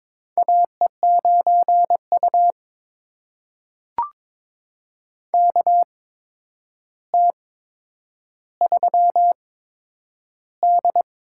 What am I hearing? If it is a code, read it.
Morse code, heard as a keyed tone, about 22 wpm, spaced as for 9 wpm, 700 Hz. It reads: AE9U KT3D
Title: USA Call Signs - 4 Characters - 8x Character Spacing - 22wpm